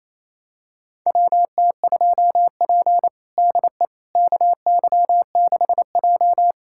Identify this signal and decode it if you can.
Morse code, heard as a keyed tone, about 28 wpm, 700 Hz. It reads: WT2P DE KY6J
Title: USA Callsign DE Callsign - 4 Characters - 28wpm